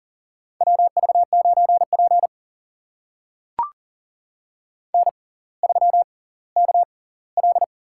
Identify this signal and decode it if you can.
Morse code, heard as a keyed tone, about 39 wpm, spaced as for 20 wpm, 700 Hz. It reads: WV9P N3KL
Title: USA Call Signs - 4 Characters - 6x Character Spacing - 40wpm